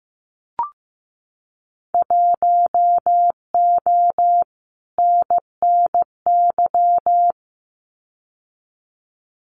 Morse code, heard as a keyed tone, about 15 wpm, 700 Hz. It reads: 1O NNY
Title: Winter Field Day Exchanges 15wpm